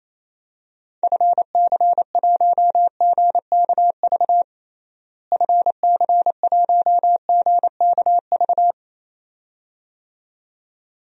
Morse code, heard as a keyed tone, about 27 wpm, 700 Hz. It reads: FC1GKV FC1GKV
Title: DXCC-100 Call Signs - 6 Characters - Repeated 2x - 28wpm